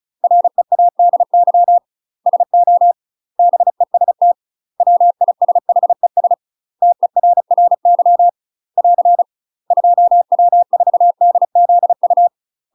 Morse code, heard as a keyed tone, about 34 wpm, 700 Hz. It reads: READY SO BEST WISHES TERRY <AR> 2W4DZU
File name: UK MASTER QSOs TRAINING / 35WPM / UK MASTER QSOs 0039